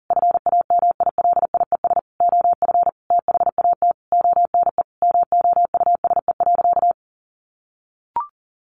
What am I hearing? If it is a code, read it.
Morse code, heard as a keyed tone, about 40 wpm, 700 Hz. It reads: FAMILIES OF THAT ONE MOUSE.
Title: A Sound Of Thunder 40wpm